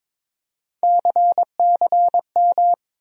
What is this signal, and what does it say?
Morse code, heard as a keyed tone, about 22 wpm, 700 Hz. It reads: CCM
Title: USA Call Sign Postfixes - 3 Letters 22wpm